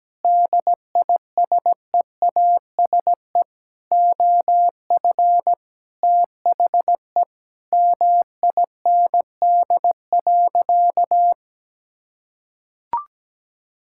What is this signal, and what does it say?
Morse code, heard as a keyed tone, about 17 wpm, 700 Hz. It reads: DISEASE OF THE MIND.
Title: Sentences from Top 1000 Words - Review Part 6 - 17wpm